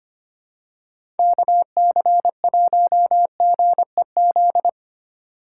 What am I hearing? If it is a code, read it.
Morse code, heard as a keyed tone, about 25 wpm, 700 Hz. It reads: KC1GEZ
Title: USA Call Signs - 6 Characters - 2x Character Spacing - 25wpm